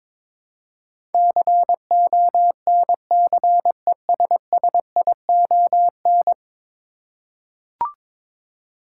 Morse code, heard as a keyed tone, about 22 wpm, 700 Hz. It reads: CONCESSION
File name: practice / 22wpm / 10 Letter Words - First 3 Letters Given - 22wpm